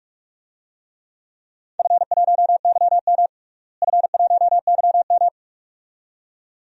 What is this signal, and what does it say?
Morse code, heard as a keyed tone, about 46 wpm, 700 Hz. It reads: F1YM F1YM
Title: DXCC-100 Call Signs - 4 Characters - Repeated 2x - 45wpm